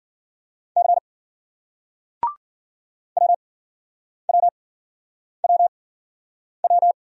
Morse code, heard as a keyed tone, about 53 wpm, spaced as for 9 wpm, 700 Hz. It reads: C WWWW